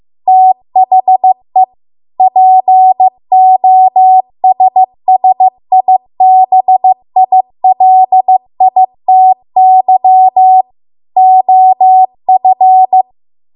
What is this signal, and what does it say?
Morse code, heard as a keyed tone, about 15 wpm, 750 Hz. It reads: THE POSSIBILITY OF